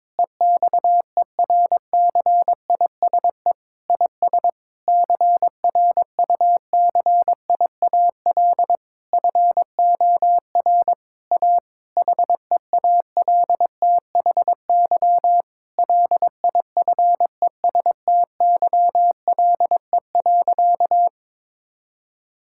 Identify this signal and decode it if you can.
Morse code, heard as a keyed tone, about 22 wpm, 700 Hz. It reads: EXERCISE IS CRUCIAL FOR A HEALTHY LIFESTYLE.
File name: Difficult Sentences with Hints - 22wpm